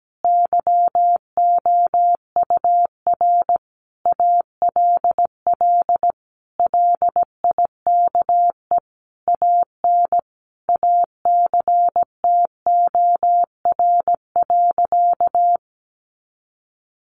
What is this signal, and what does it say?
Morse code, heard as a keyed tone, about 17 wpm, 700 Hz. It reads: YOUR ALL LIKE AN ACTOR.
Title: Sentences from Top 1500 Words 17wpm